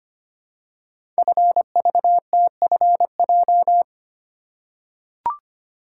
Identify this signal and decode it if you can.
Morse code, heard as a keyed tone, about 25 wpm, 700 Hz. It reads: FVTFJ
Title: Code Groups - 5 Letters - 25wpm